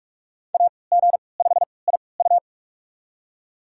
Morse code, heard as a keyed tone, about 46 wpm, spaced as for 26 wpm, 700 Hz. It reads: AG5IU
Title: USA Call Signs - 5 Characters - 3x Character Spacing - 45wpm